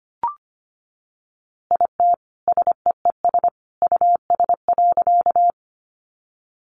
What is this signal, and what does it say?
Morse code, heard as a keyed tone, about 25 wpm, 700 Hz. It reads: IT SEES US.
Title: A Sound Of Thunder 25wpm